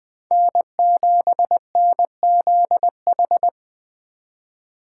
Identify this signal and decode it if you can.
Morse code, heard as a keyed tone, about 21 wpm, 700 Hz. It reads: N7NZH